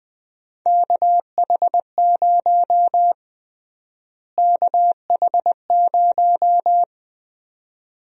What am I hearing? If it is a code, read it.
Morse code, heard as a keyed tone, about 20 wpm, 700 Hz. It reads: KH0 KH0
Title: USA Call Sign Prefixes - Repeated 2x - 20wpm